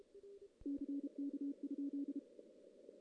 Morse code, heard as a keyed tone, about 32 wpm, 280 Hz. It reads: CK?
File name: CW / CW-SS-k5tr- -1700